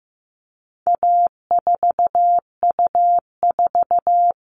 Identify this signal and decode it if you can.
Morse code, heard as a keyed tone, about 15 wpm, 700 Hz. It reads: A4U4
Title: All Cut Numbers - 4-Digit 15wpm